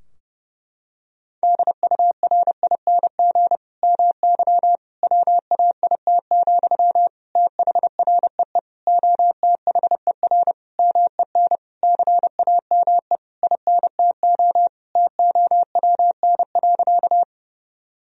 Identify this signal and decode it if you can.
Morse code, heard as a keyed tone, about 30 wpm, 700 Hz. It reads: DURING MY WAIT, THREE OTHER MEN CAME INTO TOWN.
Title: Sentences from Top 1000 Words - Review Part 2 - 30wpm